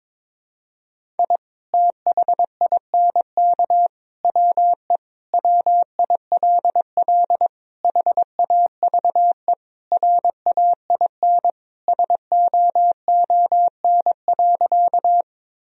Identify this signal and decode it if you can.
Morse code, heard as a keyed tone, about 22 wpm, 700 Hz. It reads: I THINK WE WILL HAVE RAIN SOON.
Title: Sentences from Top 1000 Words - Review Part 1 - 22wpm